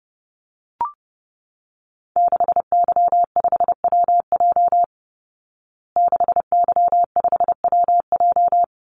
Morse code, heard as a keyed tone, about 30 wpm, 700 Hz. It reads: 6Y5WJ 6Y5WJ
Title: Call Signs Repeated 3x - 30wpm